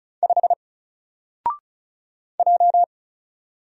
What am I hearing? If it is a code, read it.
Morse code, heard as a keyed tone, about 34 wpm, 700 Hz. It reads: 5 J
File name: Single-Letter-Number-35wpm